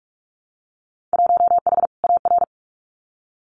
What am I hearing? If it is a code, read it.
Morse code, heard as a keyed tone, about 46 wpm, 700 Hz. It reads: 1H AR
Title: Winter Field Day Exchanges 45wpm